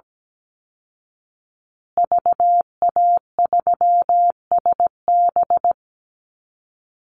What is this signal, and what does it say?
Morse code, heard as a keyed tone, about 17 wpm, 700 Hz. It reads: VA3SB
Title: Call Signs Repeated 2x - 17wpm